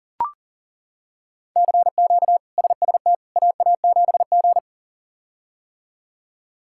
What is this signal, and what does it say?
Morse code, heard as a keyed tone, about 37 wpm, 700 Hz. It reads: CQ SST AA7G
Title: K1USN SST QSO Practice - 40wpm